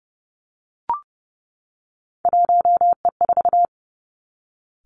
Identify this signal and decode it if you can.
Morse code, heard as a keyed tone, about 31 wpm, 700 Hz. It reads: 1E4
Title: Three Challenging Letter-Number Combinations 30wpm